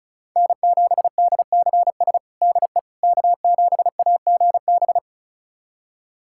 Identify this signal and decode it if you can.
Morse code, heard as a keyed tone, about 36 wpm, 700 Hz. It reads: N7DCS DE K7AGB
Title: USA Callsign DE Callsign 35wpm